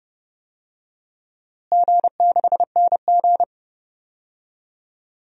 Morse code, heard as a keyed tone, about 32 wpm, 700 Hz. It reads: G6NG
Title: DXCC-100 Call Signs - 4 Characters - Repeated 2x - 30wpm